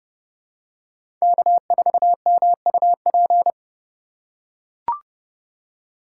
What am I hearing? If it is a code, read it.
Morse code, heard as a keyed tone, about 30 wpm, 700 Hz. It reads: K4MUP
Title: USA Call Signs - 5 Characters - 2x Character Spacing - 30wpm